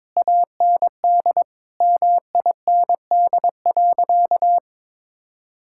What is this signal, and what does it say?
Morse code, heard as a keyed tone, about 22 wpm, 700 Hz. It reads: AND MIND.
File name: Difficult Sentences with Hints - 22wpm